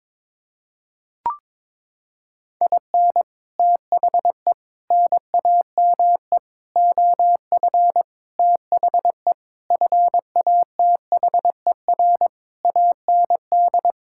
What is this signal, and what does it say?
Morse code, heard as a keyed tone, about 21 wpm, 700 Hz. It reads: IN THE NAME OF THE FATHER AND
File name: Sets of 9 Words - 22wpm